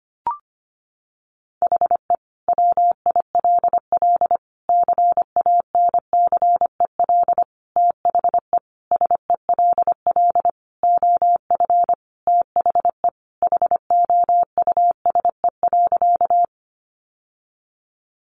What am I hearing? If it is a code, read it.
Morse code, heard as a keyed tone, about 25 wpm, 700 Hz. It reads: HE WILL CANCEL THE SELL OF THE HOUSE.
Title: Sentences from Top 1200 Words 25wpm